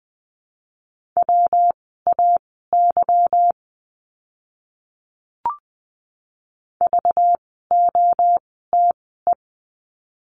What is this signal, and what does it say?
Morse code, heard as a keyed tone, about 19 wpm, spaced as for 14 wpm, 700 Hz. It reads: WAY VOTE
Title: 192 Lesson 31f - Letter v - Words 20wpm